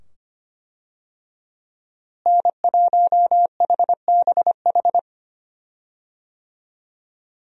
Morse code, heard as a keyed tone, about 25 wpm, 700 Hz. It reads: N1HBH
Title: USA Call Signs - 5 Characters - Repeated 2x - 25wpm